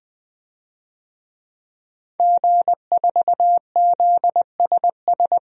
Morse code, heard as a keyed tone, about 20 wpm, 700 Hz. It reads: G4ZSS
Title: DXCC-100 Call Signs - 5 Characters - Repeated 2x - 20wpm